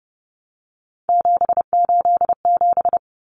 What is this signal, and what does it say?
Morse code, heard as a keyed tone, about 30 wpm, 700 Hz. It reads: 787
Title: Common Cut Numbers - 3-Digit 30wpm